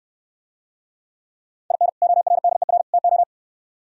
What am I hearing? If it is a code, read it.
Morse code, heard as a keyed tone, about 68 wpm, 700 Hz. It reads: IN ORDER TO